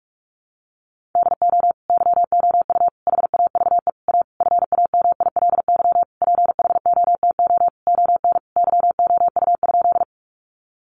Morse code, heard as a keyed tone, about 44 wpm, 700 Hz. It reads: DO YOU HAVE A FAMILY PHOTO ON YOU?
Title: Sentences from Top 1200 Words 45wpm